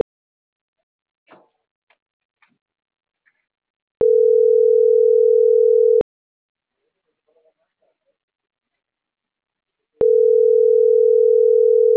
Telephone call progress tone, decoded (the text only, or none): ringback tone